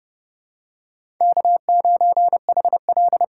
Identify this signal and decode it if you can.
Morse code, heard as a keyed tone, about 30 wpm, 700 Hz. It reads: K9HL
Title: USA Call Signs - 4 Characters - 4x Character Spacing - 30wpm